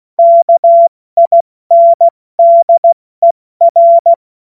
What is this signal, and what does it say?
Morse code, heard as a keyed tone, about 16 wpm, spaced as for 14 wpm, 680 Hz. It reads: KINDER
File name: Morsekurs / Witze / Witz 17. Januar 2023 70 F80